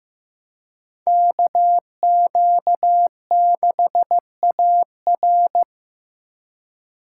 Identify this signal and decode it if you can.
Morse code, heard as a keyed tone, about 18 wpm, 700 Hz. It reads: KQ6AR
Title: Speed Racing - USA Call Signs 15wpm